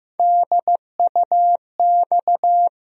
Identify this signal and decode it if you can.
Morse code, heard as a keyed tone, about 15 wpm, 700 Hz. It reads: DUX
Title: USA Call Sign Postfixes - 3 Letters 15wpm